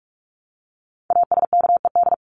Morse code, heard as a keyed tone, about 44 wpm, 700 Hz. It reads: ASKED
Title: ED Suffix - Top 100 Words with Base Words 45wpm